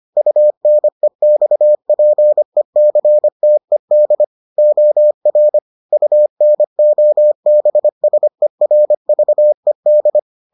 Morse code, heard as a keyed tone, about 25 wpm, 600 Hz. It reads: UNEXPECTED OR UNOBSERVED